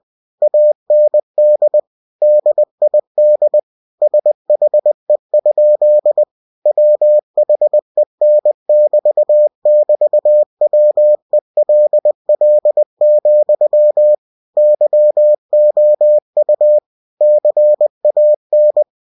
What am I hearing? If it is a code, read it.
Morse code, heard as a keyed tone, about 20 wpm, 600 Hz. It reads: AND DID SHE? WHEN <BT> <BT> WELL, YOU CAN
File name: cw-books / winnie-the-pooh / winnie-20wpm-page-0101